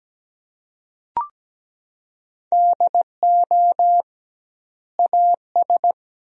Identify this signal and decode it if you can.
Morse code, heard as a keyed tone, about 17 wpm, 700 Hz. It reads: DO AS
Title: 2x Word-Spacing - Sentences from Top 100 Words - 17wpm